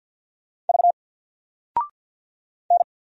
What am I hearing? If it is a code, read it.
Morse code, heard as a keyed tone, about 50 wpm, 700 Hz. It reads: V N
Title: Single Character (Letter:Number:Punctuation:Prosign) 50wpm